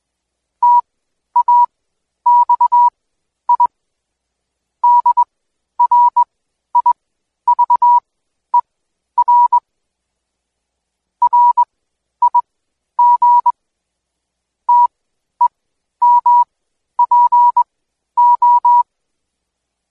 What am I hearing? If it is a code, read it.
Morse code, heard as a keyed tone, about 21 wpm, spaced as for 11 wpm, 970 Hz. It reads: TAXI DRIVER RIG TEMPO
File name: Code Quick / AudioDownld2 / Seg120-Day 25-DayDrill 25